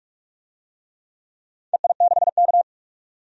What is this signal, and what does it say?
Morse code, heard as a keyed tone, about 46 wpm, 700 Hz. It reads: EI6K
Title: DXCC-100 Call Signs - 4 Characters - 45wpm